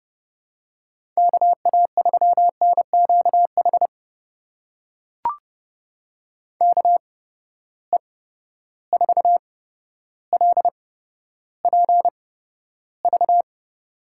Morse code, heard as a keyed tone, about 32 wpm, spaced as for 13 wpm, 700 Hz. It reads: KA3NQH KE4LPV
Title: USA Call Signs - 6 Characters - 8x Character Spacing - 30wpm